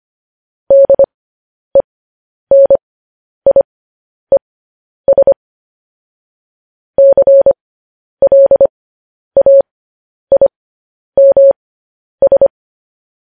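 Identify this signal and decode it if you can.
Morse code, heard as a keyed tone, about 25 wpm, spaced as for 10 wpm, 550 Hz. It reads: DENIES CLAIMS
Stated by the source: Morse news headlines at 10.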